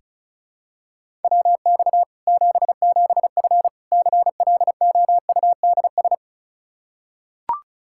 Morse code, heard as a keyed tone, about 36 wpm, 700 Hz. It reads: WX 77F CLOUDS
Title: Practice QSO - Exchange 2 - Info - 35wpm